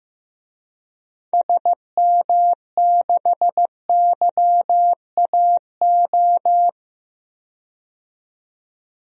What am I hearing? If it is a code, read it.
Morse code, heard as a keyed tone, about 16 wpm, 700 Hz. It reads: SM6YAO
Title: DXCC-100 Call Signs - 6 Characters - 15wpm